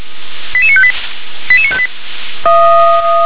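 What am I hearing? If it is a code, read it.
Five-tone selective call (ZVEI), heard as quick runs of five tones, tones of about 70 ms. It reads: 88958 88958
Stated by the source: ZVEI five-tone selective call received on SDR.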